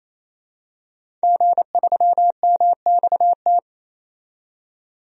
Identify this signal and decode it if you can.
Morse code, heard as a keyed tone, about 28 wpm, 700 Hz. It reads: G3MXT
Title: DXCC-100 Call Signs - 5 Characters - 28wpm